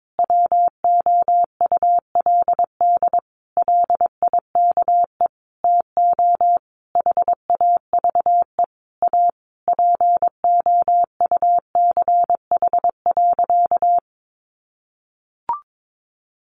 Morse code, heard as a keyed tone, about 22 wpm, 700 Hz. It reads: WOULD LIKE TO HAVE A POUCH.
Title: Easy Phrases 22wpm